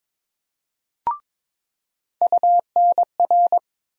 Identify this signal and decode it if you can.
Morse code, heard as a keyed tone, about 22 wpm, 700 Hz. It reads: UNR
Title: USA Call Sign Postfixes - 3 Letters 22wpm